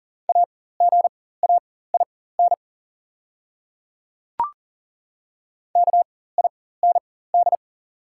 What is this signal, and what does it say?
Morse code, heard as a keyed tone, about 40 wpm, spaced as for 19 wpm, 690 Hz. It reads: AGAIN KIND
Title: 4x Character Spacing - Top 200 Words - 40wpm